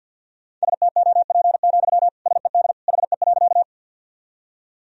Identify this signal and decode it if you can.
Morse code, heard as a keyed tone, about 52 wpm, 700 Hz. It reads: STOP, SED HE.
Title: Sentences from Top 300 Words and Abbreviations 50wpm